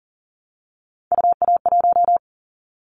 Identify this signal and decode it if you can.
Morse code, heard as a keyed tone, about 42 wpm, 700 Hz. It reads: UA1
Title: Callsign Prefixes for Top 50 DXCC - 40wpm